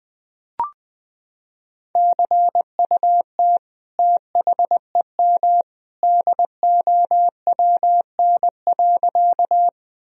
Morse code, heard as a keyed tone, about 21 wpm, 700 Hz. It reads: CUT THEM DOWN.
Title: Sentences from Top 1000 Words - Review Part 1 - 20wpm